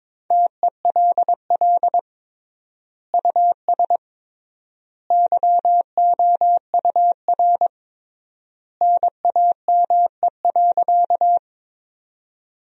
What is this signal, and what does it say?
Morse code, heard as a keyed tone, about 21 wpm, 700 Hz. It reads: TELL US YOUR NAME.
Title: 3x Word-Spacing - Sentences from Top 200 Words - 22wpm